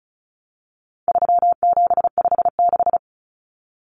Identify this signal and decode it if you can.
Morse code, heard as a keyed tone, about 34 wpm, 700 Hz. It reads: 3756
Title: Common Cut Numbers - 4-Digit 35wpm